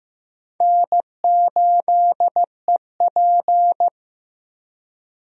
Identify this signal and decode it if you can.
Morse code, heard as a keyed tone, about 15 wpm, 700 Hz. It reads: N8EP
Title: USA Call Signs - 4 Characters - 15wpm